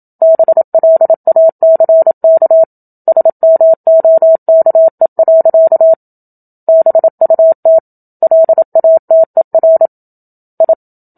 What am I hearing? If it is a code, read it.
Morse code, heard as a keyed tone, about 27 wpm, 650 Hz. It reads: BLACK SMOKE. BUT LATER I
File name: Book Two / 01-Chapter 1 - 27 WPM